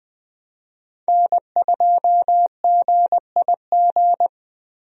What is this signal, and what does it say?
Morse code, heard as a keyed tone, about 20 wpm, 700 Hz. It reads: N2GIG
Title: USA Call Signs - 5 Characters - Repeated 3x - 20wpm